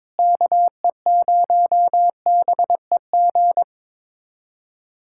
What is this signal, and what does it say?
Morse code, heard as a keyed tone, about 21 wpm, 700 Hz. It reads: KE0BEG
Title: USA Callsign DE Callsign 22wpm